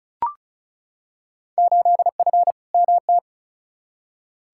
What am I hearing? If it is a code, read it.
Morse code, heard as a keyed tone, about 36 wpm, 700 Hz. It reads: 8F MT
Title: Field Day Exchanges Encore - 35wpm